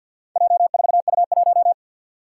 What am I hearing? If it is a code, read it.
Morse code, heard as a keyed tone, about 50 wpm, 700 Hz. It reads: J4U1
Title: Four Challenging Letter-Number Combinations 50wpm